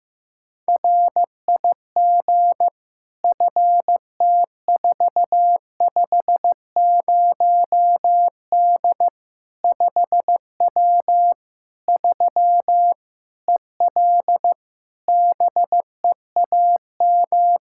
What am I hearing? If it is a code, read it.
Morse code, heard as a keyed tone, about 15 wpm, 700 Hz. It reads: RIG FT450D 5W 3 EL BEAM
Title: Practice QSO - Exchange 2 - Info - 15wpm